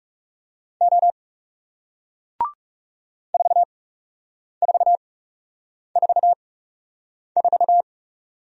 Morse code, heard as a keyed tone, about 40 wpm, spaced as for 8 wpm, 700 Hz. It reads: O 4444